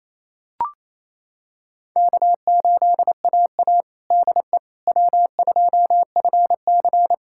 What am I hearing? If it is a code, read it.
Morse code, heard as a keyed tone, about 28 wpm, 700 Hz. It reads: K8AA DE W2FC
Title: USA Callsign DE Callsign - 4 Characters - 28wpm